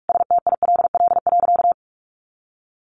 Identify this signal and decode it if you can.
Morse code, heard as a keyed tone, about 44 wpm, 700 Hz. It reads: STILL.
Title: Sentences from Top 500 Words 45wpm